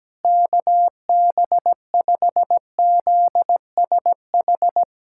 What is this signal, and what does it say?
Morse code, heard as a keyed tone, about 17 wpm, 700 Hz. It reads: KB5ZSH
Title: USA Callsign DE Callsign - 6 Characters - 17wpm